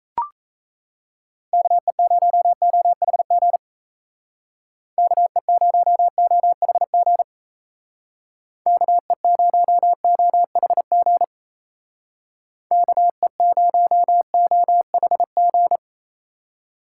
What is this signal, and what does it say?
Morse code, heard as a keyed tone, about 37 wpm, 700 Hz. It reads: KE0OHG KE0OHG KE0OHG KE0OHG